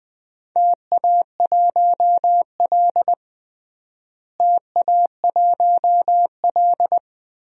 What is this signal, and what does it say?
Morse code, heard as a keyed tone, about 20 wpm, 700 Hz. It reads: TA1L TA1L
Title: DXCC-100 Call Signs - 4 Characters - Repeated 2x - 20wpm